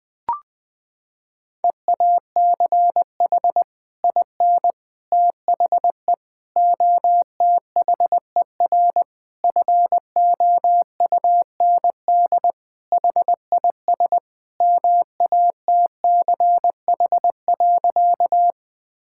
Morse code, heard as a keyed tone, about 20 wpm, 700 Hz. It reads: EACH IN THE OTHER FOUND HIS MATCH.
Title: Sentences from Top 1600 Words 20wpm